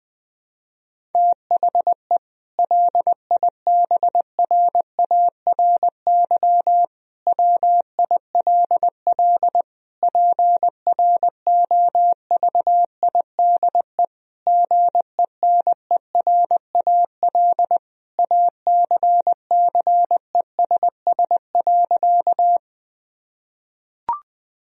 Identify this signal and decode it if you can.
Morse code, heard as a keyed tone, about 21 wpm, 700 Hz. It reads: THE LIBRARY WILL PROVIDE GENERAL ACCESS.
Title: Sentences from Top 1000 Words - Review Part 4 - 20wpm